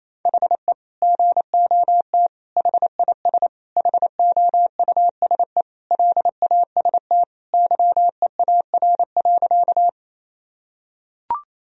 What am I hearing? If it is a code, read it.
Morse code, heard as a keyed tone, about 28 wpm, 700 Hz. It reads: HE GOT HIS HOUSE LAST YEAR.
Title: Sentences from Top 1000 Words - Review Part 1 - 28wpm